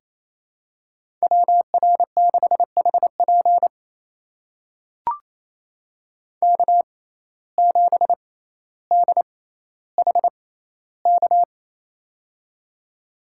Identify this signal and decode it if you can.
Morse code, heard as a keyed tone, about 27 wpm, spaced as for 14 wpm, 700 Hz. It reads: WR6HP K7DHK
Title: USA Call Signs - 5 Characters - 6x Character Spacing - 28wpm